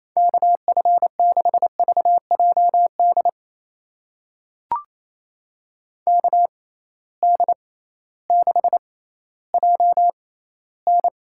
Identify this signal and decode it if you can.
Morse code, heard as a keyed tone, about 27 wpm, spaced as for 15 wpm, 700 Hz. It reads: KF6VJD KD6JN
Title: USA Call Signs - 6 Characters - 6x Character Spacing - 28wpm